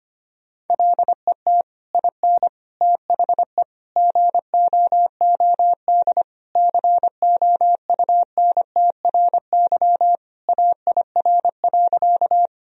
Morse code, heard as a keyed tone, about 25 wpm, 700 Hz. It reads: LET IN THE GOOD COUNTRY AIR.
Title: Sentences from Top 1000 Words - Review Part 1 - 25wpm